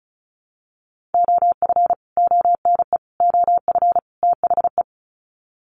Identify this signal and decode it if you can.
Morse code, heard as a keyed tone, about 34 wpm, 700 Hz. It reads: OF ONE OF THE
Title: Sets of 4 Words 35wpm